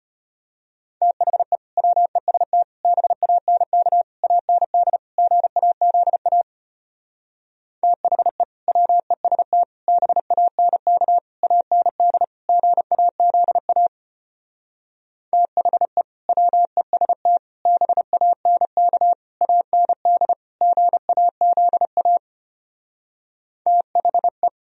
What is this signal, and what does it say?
Morse code, heard as a keyed tone, about 33 wpm, 700 Hz. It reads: THE WEST BANK AND GAZA THE WEST BANK AND GAZA THE WEST BANK AND GAZA THE